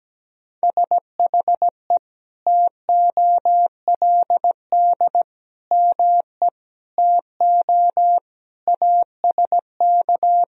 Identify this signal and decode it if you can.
Morse code, heard as a keyed tone, about 17 wpm, 700 Hz. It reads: SHE TOLD ME TO ASK
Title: Sentences from Top 400 Words and Abbreviations 17wpm